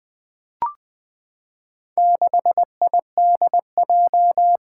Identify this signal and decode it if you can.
Morse code, heard as a keyed tone, about 20 wpm, 700 Hz. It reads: 6IDJ
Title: Four Challenging Letter-Number Combinations 20wpm